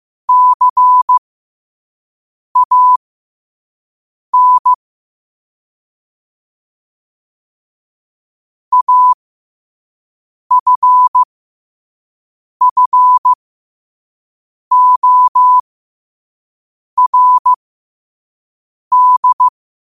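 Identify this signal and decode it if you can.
Morse code, heard as a keyed tone, about 15 wpm, spaced as for 5 wpm, 1000 Hz. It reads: CAN AFFORD
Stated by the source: Morse code quote of the day at 5 wpm.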